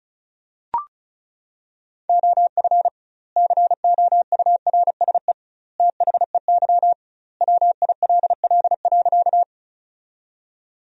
Morse code, heard as a keyed tone, about 36 wpm, 700 Hz. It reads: OF COURSE THEY WILL.
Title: 2x Word-Spacing - Sentences from Top 500 Words - 35wpm